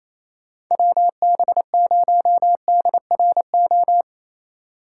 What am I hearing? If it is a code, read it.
Morse code, heard as a keyed tone, about 28 wpm, 700 Hz. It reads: WB0DRO
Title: USA Call Signs - 6 Characters - 6x Character Spacing - 28wpm